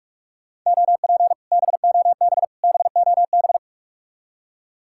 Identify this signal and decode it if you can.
Morse code, heard as a keyed tone, about 58 wpm, 700 Hz. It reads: OP BOB BOB